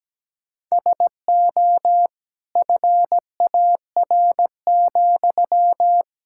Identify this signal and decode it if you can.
Morse code, heard as a keyed tone, about 17 wpm, 700 Hz. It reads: SO FAR,
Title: Sentences from Top 1000 Words - Review Part 6 - 17wpm